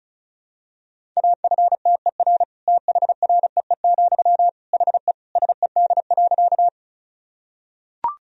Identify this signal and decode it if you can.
Morse code, heard as a keyed tone, about 36 wpm, 700 Hz. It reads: AFTER THREE, HE SED.
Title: Sentences from Top 200 Words and Abbreviations 35wpm